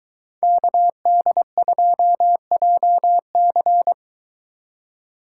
Morse code, heard as a keyed tone, about 23 wpm, 700 Hz. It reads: KD2JC